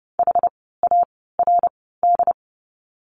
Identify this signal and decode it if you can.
Morse code, heard as a keyed tone, about 30 wpm, spaced as for 17 wpm, 700 Hz. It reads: HARD
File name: Lesson 16f - Letter c - Words 30wpm